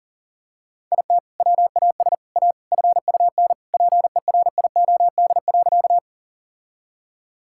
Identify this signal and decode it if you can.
Morse code, heard as a keyed tone, about 40 wpm, 700 Hz. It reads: IT WAS A FUN PERIOD.
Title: Sentences from Top 1000 Words - Review Part 5 - 40wpm